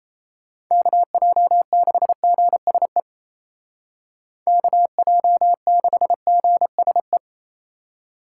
Morse code, heard as a keyed tone, about 34 wpm, 700 Hz. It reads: KJ6GSE KJ6GSE
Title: Speed Racing - USA 2x3 Call Signs 28wpm